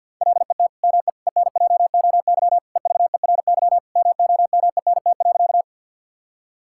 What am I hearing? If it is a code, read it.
Morse code, heard as a keyed tone, about 50 wpm, 700 Hz. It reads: LET ME ENJOY EVERY MOMENT.
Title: Sentences from Top 1000 Words - Review Part 5 - 50wpm